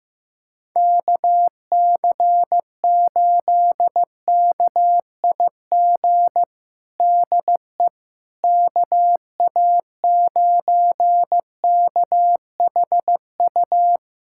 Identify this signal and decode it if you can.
Morse code, heard as a keyed tone, about 15 wpm, 700 Hz. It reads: KC8KIG DE KA9KHU
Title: USA Callsign DE Callsign - 6 Characters - 15wpm